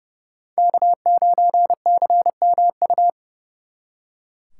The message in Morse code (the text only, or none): K9CMU